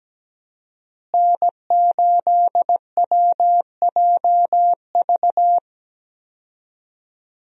Morse code, heard as a keyed tone, about 17 wpm, 700 Hz. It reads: N8WJV
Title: USA Call Signs - 5 Characters - Repeated 3x - 17wpm